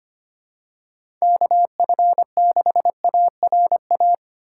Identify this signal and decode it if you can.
Morse code, heard as a keyed tone, about 25 wpm, 700 Hz. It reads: KF6ARA
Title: USA Call Signs - 6 Characters - 3x Character Spacing - 25wpm